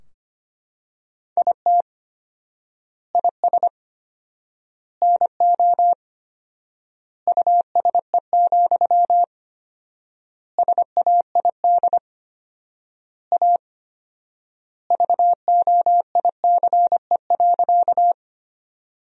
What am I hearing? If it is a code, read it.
Morse code, heard as a keyed tone, about 25 wpm, 700 Hz. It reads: IT IS NO USE, SAID A VOICE.
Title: 4x Word-Spacing - Sentences from Top 500 Words - 25wpm